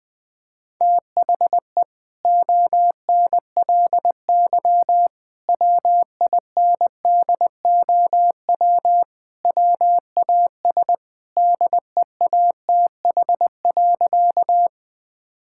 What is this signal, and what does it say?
Morse code, heard as a keyed tone, about 20 wpm, 700 Hz. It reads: THE ONLY WINDOW WAS DEATH.
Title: Sentences from Top 1600 Words 20wpm